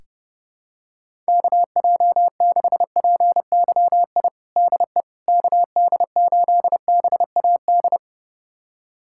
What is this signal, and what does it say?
Morse code, heard as a keyed tone, about 30 wpm, 700 Hz. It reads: KJ6PYI DE KD8BAD
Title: USA Callsign DE Callsign - 6 Characters - 30wpm